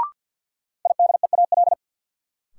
Morse code, heard as a keyed tone, about 52 wpm, 700 Hz. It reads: IDEAL
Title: Sentences from Top 1400 Words - New Word Drill - 50wpm